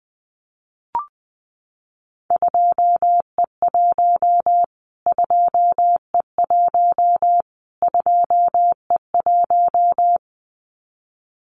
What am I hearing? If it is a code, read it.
Morse code, heard as a keyed tone, about 20 wpm, 700 Hz. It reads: 2E1 2E1 2E1
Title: Callsign Prefixes for Top 25 DXCC - Repeated 3x - 20wpm